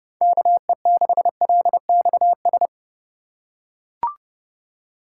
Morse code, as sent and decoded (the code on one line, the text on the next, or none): -.- . -.... .-.. -..- ...
KE6LXS